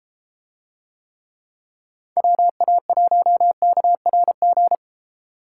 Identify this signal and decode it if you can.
Morse code, heard as a keyed tone, about 32 wpm, 700 Hz. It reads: WA1KRG